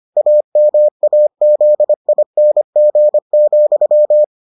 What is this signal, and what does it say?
Morse code, heard as a keyed tone, about 25 wpm, 600 Hz. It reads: AMAZING,